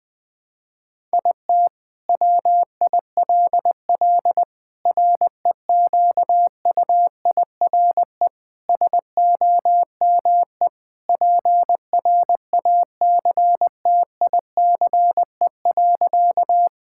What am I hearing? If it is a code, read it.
Morse code, heard as a keyed tone, about 20 wpm, 700 Hz. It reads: IT WILL REQUIRE SOME PRACTICE.
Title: Sentences from Top 1000 Words - Review Part 6 - 20wpm